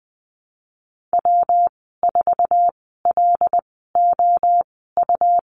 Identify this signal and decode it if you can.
Morse code, heard as a keyed tone, about 20 wpm, spaced as for 15 wpm, 700 Hz. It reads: W4LOU
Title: 186 Lesson 30g - Letter b - Call Signs 20wpm